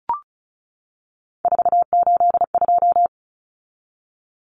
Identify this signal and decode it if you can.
Morse code, heard as a keyed tone, about 34 wpm, 700 Hz. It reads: 482
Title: Common Cut Numbers - 3-Digit 35wpm